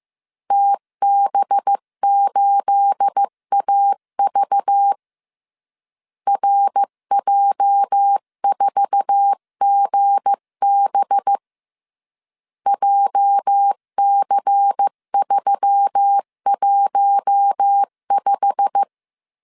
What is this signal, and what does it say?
Morse code, heard as a keyed tone, about 15 wpm, 800 Hz. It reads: TB8AV RJ4GB JC315
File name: CW / DET / 1 9 1500C S B 14wpm 3302